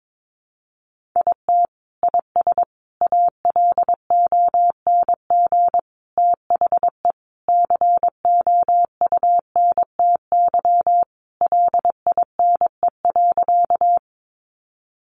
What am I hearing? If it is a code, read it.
Morse code, heard as a keyed tone, about 22 wpm, 700 Hz. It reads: IT IS ALONG THE COUNTY LINE.
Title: Sentences from Top 1200 Words 22wpm